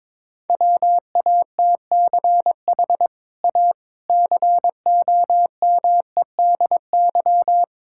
Morse code, heard as a keyed tone, about 21 wpm, 700 Hz. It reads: WATCH A COMEDY
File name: practice / 22wpm / Easy Sentences with Hints - 22wpm